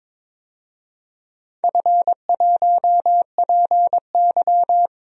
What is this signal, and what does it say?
Morse code, heard as a keyed tone, about 22 wpm, 700 Hz. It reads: F1PY
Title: DXCC-100 Call Signs - 4 Characters - 22wpm